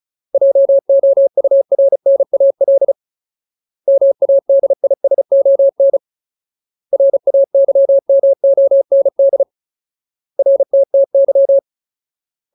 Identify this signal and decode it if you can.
Morse code, heard as a keyed tone, about 34 wpm, 550 Hz. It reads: JOURNAL MADISON RAYMOND RTTY
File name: Ultimate Word List / 35WPM Ultimate